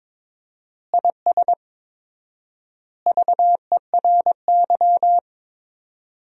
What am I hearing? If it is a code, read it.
Morse code, heard as a keyed tone, about 21 wpm, 700 Hz. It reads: IS VERY
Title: 4x Word-Spacing - Sentences from Top 400 Words - 22wpm